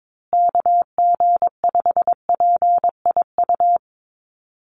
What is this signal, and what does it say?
Morse code, heard as a keyed tone, about 22 wpm, 700 Hz. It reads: KG5PIU
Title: USA Call Signs Repeated 3x 22wpm